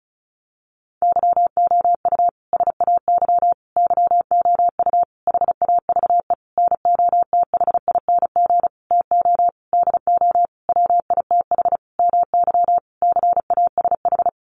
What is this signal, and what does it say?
Morse code, heard as a keyed tone, about 34 wpm, 700 Hz. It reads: YOU SAY YOU HAVE NOTHING TO DO WITH MY CASH
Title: Sentences from Top 700 Words 35wpm